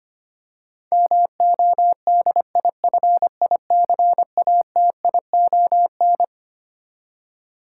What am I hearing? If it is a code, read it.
Morse code, heard as a keyed tone, about 25 wpm, 700 Hz. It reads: MODIFICATION
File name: Twelve Letter Words - 25wpm